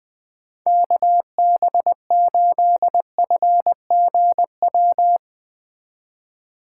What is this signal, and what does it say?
Morse code, heard as a keyed tone, about 20 wpm, 700 Hz. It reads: KB8FGW
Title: Speed Racing - USA 2x3 Call Signs 17wpm